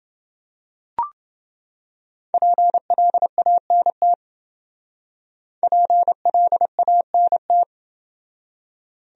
Morse code, heard as a keyed tone, about 28 wpm, 700 Hz. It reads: PLANT PLANT